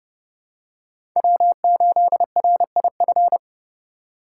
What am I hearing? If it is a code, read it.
Morse code, heard as a keyed tone, about 30 wpm, 700 Hz. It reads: W8RIF
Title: USA Call Signs - 5 Characters - 3x Character Spacing - 30wpm